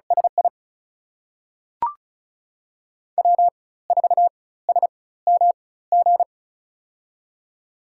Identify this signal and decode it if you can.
Morse code, heard as a keyed tone, about 36 wpm, spaced as for 18 wpm, 700 Hz. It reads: SI W4SMG
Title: USA Call Signs - 5 Characters - 4x Character Spacing - 35wpm